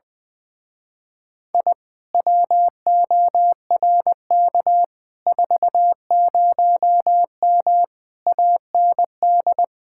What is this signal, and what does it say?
Morse code, heard as a keyed tone, about 20 wpm, 700 Hz. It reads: I WORK 40M AND
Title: Speed-Racing Common QSO Phrases - 20wpm